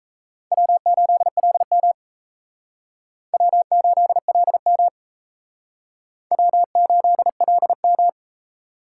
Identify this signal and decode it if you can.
Morse code, heard as a keyed tone, about 35 wpm, 700 Hz. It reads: W8LM W8LM W8LM
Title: Speed Racing - USA 1x2 Call Signs 28wpm